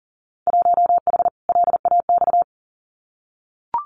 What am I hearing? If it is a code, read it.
Morse code, heard as a keyed tone, about 40 wpm, 700 Hz. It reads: 1H LAX
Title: Winter Field Day Exchanges 40wpm